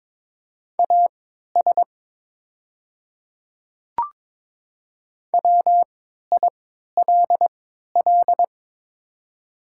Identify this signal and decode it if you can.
Morse code, heard as a keyed tone, about 22 wpm, spaced as for 13 wpm, 690 Hz. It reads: AS WILL
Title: 3x Character Spacing - Top 100 Words - 22wpm